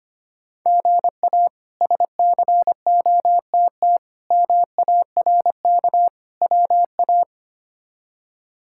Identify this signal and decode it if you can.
Morse code, heard as a keyed tone, about 25 wpm, 700 Hz. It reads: GA SCOTT MARK WA
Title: K1USN SST QSO Practice - 25wpm